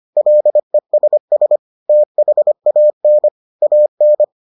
Morse code, heard as a keyed tone, about 25 wpm, 600 Hz. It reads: LESS THAN AN